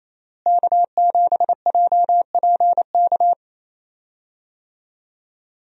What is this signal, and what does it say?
Morse code, heard as a keyed tone, about 28 wpm, 700 Hz. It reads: K7JPK
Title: USA Call Signs - 5 Characters - Repeated 2x - 28wpm